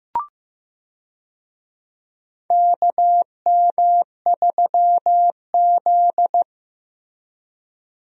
Morse code, heard as a keyed tone, about 15 wpm, 700 Hz. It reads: KM3Z
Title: USA Call Signs - 4 Characters - Repeated 2x - 15wpm